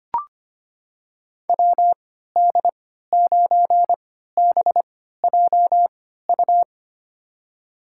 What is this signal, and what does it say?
Morse code, heard as a keyed tone, about 25 wpm, spaced as for 14 wpm, 700 Hz. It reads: WD9BJU
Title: USA Call Signs - 6 Characters - 3x Character Spacing - 25wpm